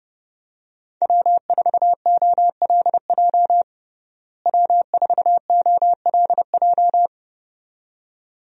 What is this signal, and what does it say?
Morse code, heard as a keyed tone, about 32 wpm, 700 Hz. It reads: W4OLJ W4OLJ
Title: USA Call Signs - 5 Characters - Repeated 2x - 30wpm